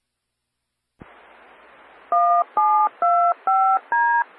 DTMF, heard as telephone touch tones, tones of about 320 ms. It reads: 1725D
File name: Central Arkansas Radio Emergency Net (CAREN) 146940 MHz Repeater